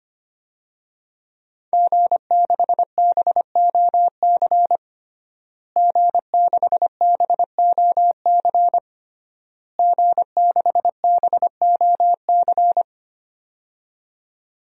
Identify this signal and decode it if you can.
Morse code, heard as a keyed tone, about 25 wpm, 700 Hz. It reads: G6BOC G6BOC G6BOC
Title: DXCC-100 Call Signs - 5 Characters - Repeated 3x - 25wpm